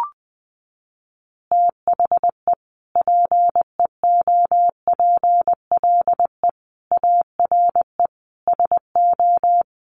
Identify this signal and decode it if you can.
Morse code, heard as a keyed tone, about 20 wpm, 700 Hz. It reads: THE PEOPLE ARE SO
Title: Additional Easy Phrases 20wpm